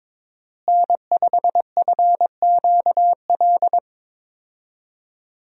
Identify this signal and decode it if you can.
Morse code, heard as a keyed tone, about 22 wpm, 700 Hz. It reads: N5FQL
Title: USA Call Signs - 5 Characters - Repeated 2x - 22wpm